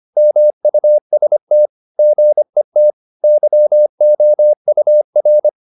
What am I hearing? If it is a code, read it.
Morse code, heard as a keyed tone, about 25 wpm, 600 Hz. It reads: MUST GET YOUR
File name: the old man and the sea part 1 25wpm